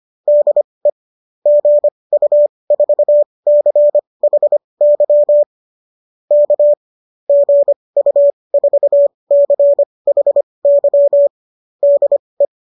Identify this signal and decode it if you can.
Morse code, heard as a keyed tone, about 25 wpm, spaced as for 20 wpm, 590 Hz. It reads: DE GU4CHY K GU4CHY DE